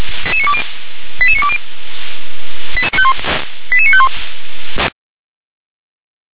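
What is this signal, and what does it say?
Five-tone selective call (ZVEI), heard as quick runs of five tones, tones of about 70 ms. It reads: 88920 88951
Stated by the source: ZVEI five-tone call received by SDR.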